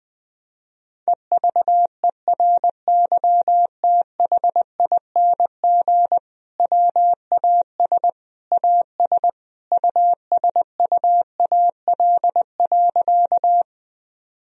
Morse code, heard as a keyed tone, about 20 wpm, 700 Hz. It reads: EVERYTHING WAS AS USUAL.
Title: Sentences from Top 1000 Words - Review Part 2 - 20wpm